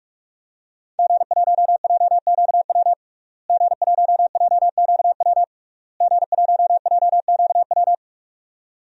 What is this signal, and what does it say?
Morse code, heard as a keyed tone, about 42 wpm, 700 Hz. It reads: G1JQW G1JQW G1JQW
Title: DXCC-100 Call Signs - 5 Characters - Repeated 3x - 45wpm